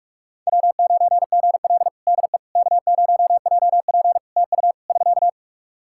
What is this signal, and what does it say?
Morse code, heard as a keyed tone, about 44 wpm, 700 Hz. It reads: W9GL DE K0JP TU <SK>
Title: Practice QSO - Exchange 4 - Ending - 45wpm